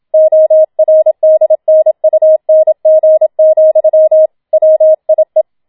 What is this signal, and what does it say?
Morse code, heard as a keyed tone, about 27 wpm, 630 Hz. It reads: ORDNUNG, WIE
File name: Schneewittchen Teil2 27wpm